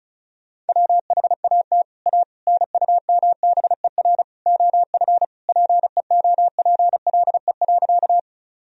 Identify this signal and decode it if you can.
Morse code, heard as a keyed tone, about 34 wpm, 700 Hz. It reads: WHAT A NUMBER OF PEOPLE.
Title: Sentences from Top 1000 Words - Review Part 1 - 35wpm